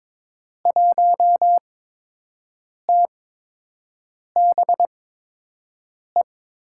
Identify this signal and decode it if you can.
Morse code, heard as a keyed tone, about 22 wpm, 700 Hz. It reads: 1 T B E
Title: USA Call Signs - 6 Characters - 8x Character Spacing - 22wpm